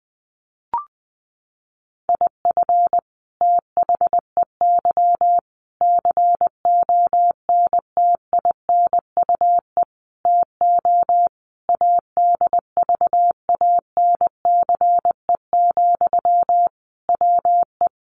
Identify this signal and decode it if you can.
Morse code, heard as a keyed tone, about 20 wpm, 700 Hz. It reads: IF THEY CONTINUE TO ADVANCE, WE
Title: Sentences from Top 1400 Words 20wpm